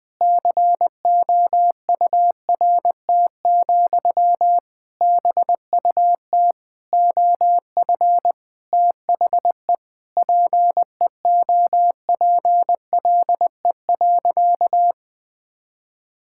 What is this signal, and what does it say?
Morse code, heard as a keyed tone, about 20 wpm, 700 Hz. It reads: COURT, BUT OF THE PEOPLE.
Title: Sentences from Top 1600 Words 20wpm